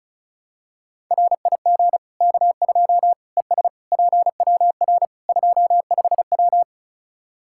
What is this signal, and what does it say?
Morse code, heard as a keyed tone, about 36 wpm, 700 Hz. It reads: RIG K2 ES PWR 25W
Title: Practice QSO - Exchange 2 - Info - 35wpm